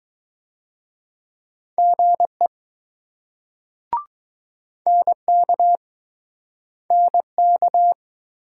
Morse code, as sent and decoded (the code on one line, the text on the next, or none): --. / -. -.- / -. -.-
GE NK NK